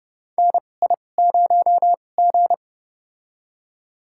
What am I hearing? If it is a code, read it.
Morse code, heard as a keyed tone, about 30 wpm, spaced as for 22 wpm, 700 Hz. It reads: NI0G
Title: USA Call Signs - 2x Character Spacing - 30wpm